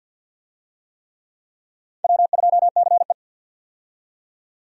Morse code, heard as a keyed tone, about 49 wpm, 700 Hz. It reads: W2CE
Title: USA Call Signs - 4 Characters - 50wpm